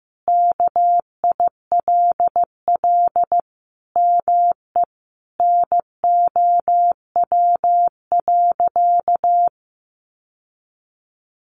Morse code, heard as a keyed tone, about 15 wpm, 700 Hz. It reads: KILL ME NOW.
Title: Sentences from Top 1400 Words 15wpm